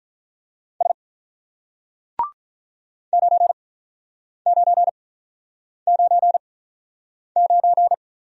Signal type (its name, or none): Morse code